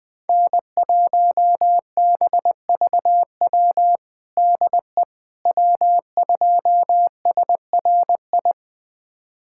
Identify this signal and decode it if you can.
Morse code, heard as a keyed tone, about 20 wpm, 700 Hz. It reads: N1BVW DE W2SRI
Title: USA Callsign DE Callsign - 5 Characters - 20wpm